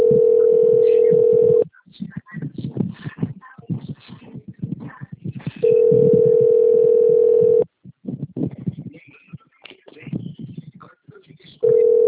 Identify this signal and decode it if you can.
Ringback tone (North America), 440 + 480 Hz.
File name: rg-1600-unknown- - - .5622